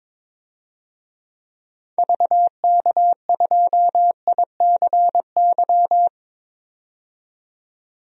Morse code, heard as a keyed tone, about 22 wpm, 700 Hz. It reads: VK2ICY
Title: DXCC-100 Call Signs - 6 Characters - 22wpm